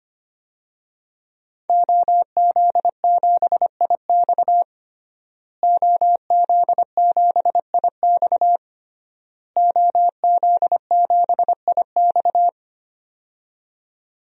Morse code, heard as a keyed tone, about 25 wpm, 700 Hz. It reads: OZ7IX OZ7IX OZ7IX
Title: DXCC-100 Call Signs - 5 Characters - Repeated 3x - 25wpm